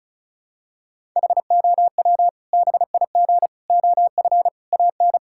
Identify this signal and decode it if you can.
Morse code, heard as a keyed tone, about 36 wpm, 700 Hz. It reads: HOW BIG OF AN
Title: Sentences from Top 1000 Words - Review Part 6 - 35wpm